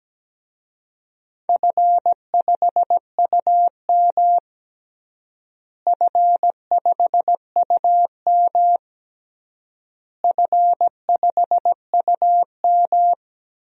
Morse code, heard as a keyed tone, about 17 wpm, 700 Hz. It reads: F5UM F5UM F5UM
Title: DXCC-100 Call Signs - 4 Characters - Repeated 3x - 17wpm